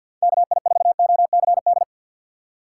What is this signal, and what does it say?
Morse code, heard as a keyed tone, about 50 wpm, 700 Hz. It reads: KI4OCD
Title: USA Call Signs - 6 Characters - 3x Character Spacing - 50wpm